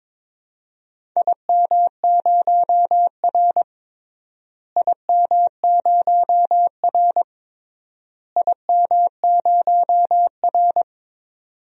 Morse code, heard as a keyed tone, about 21 wpm, 700 Hz. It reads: IM0R IM0R IM0R
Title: DXCC-100 Call Signs - 4 Characters - Repeated 3x - 22wpm